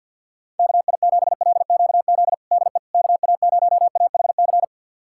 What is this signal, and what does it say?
Morse code, heard as a keyed tone, about 47 wpm, 700 Hz. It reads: KI7LQZ DE KA0AHC
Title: USA Callsign DE Callsign 50wpm